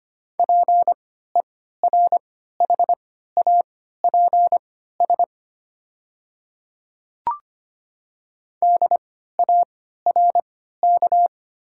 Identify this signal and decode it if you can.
Morse code, heard as a keyed tone, about 25 wpm, spaced as for 14 wpm, 700 Hz. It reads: PERHAPS DARK
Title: 3x Character Spacing - Top 500 Words - 25wpm